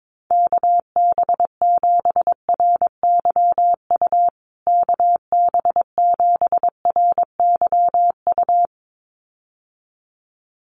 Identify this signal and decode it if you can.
Morse code, heard as a keyed tone, about 22 wpm, 700 Hz. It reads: KB7RYU KB7RYU
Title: USA Call Signs Repeated 2x 22wpm